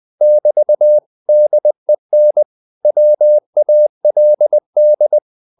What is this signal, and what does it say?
Morse code, heard as a keyed tone, about 20 wpm, 600 Hz. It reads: <BT> DEN WALD